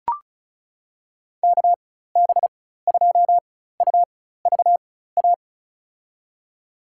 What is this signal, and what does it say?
Morse code, heard as a keyed tone, about 34 wpm, spaced as for 16 wpm, 700 Hz. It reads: KB2UVA
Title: USA Call Signs - 6 Characters - 4x Character Spacing - 35wpm